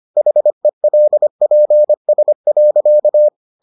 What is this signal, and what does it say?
Morse code, heard as a keyed tone, about 25 wpm, 600 Hz. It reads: HELPS.